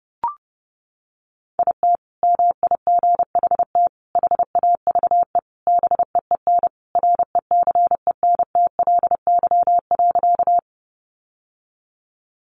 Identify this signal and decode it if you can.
Morse code, heard as a keyed tone, about 30 wpm, 700 Hz. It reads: IT MIGHT HAVE BEEN RECENTLY.
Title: Sentences from Top 1200 Words 30wpm